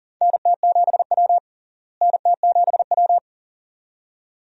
Morse code, heard as a keyed tone, about 40 wpm, 700 Hz. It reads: NT7W NT7W
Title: USA Call Signs - 4 Characters - Repeated 2x - 40wpm